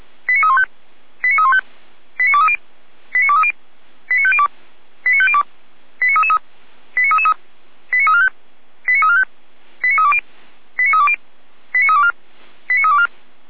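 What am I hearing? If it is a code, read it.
Five-tone selective call (ZVEI), heard as quick runs of five tones, tones of about 70 ms. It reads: 89315 89315 89230 89230 89502 89502 89333 89333 89345 89345 89320 89320 89324 89324
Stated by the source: ZVEI five-tone call 89322 received on an SDR.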